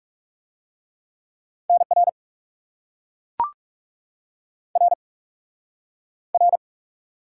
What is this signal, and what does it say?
Morse code, heard as a keyed tone, about 40 wpm, 700 Hz. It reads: NR R R